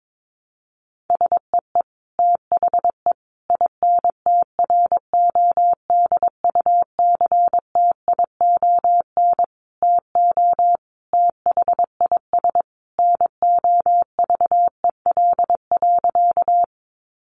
Morse code, heard as a keyed tone, about 22 wpm, 700 Hz. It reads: SEE THE INTRODUCTION TO THIS NOVEL.
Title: Sentences from Top 1300 Words 22wpm